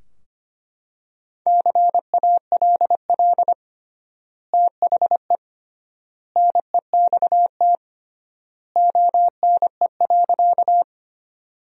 Morse code, heard as a keyed tone, about 25 wpm, 700 Hz. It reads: CALL THE NEXT ONE.
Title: 3x Word-Spacing - Sentences from Top 300 Words - 25wpm